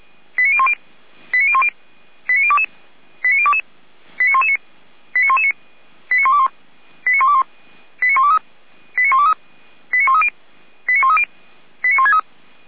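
Five-tone selective call (ZVEI), heard as quick runs of five tones, tones of about 70 ms. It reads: 89010 89010 89022 89022 89119 89119 89121 89121 89123 89123 89130 89130 89163
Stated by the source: ZVEI five-tone selective call received by SDR.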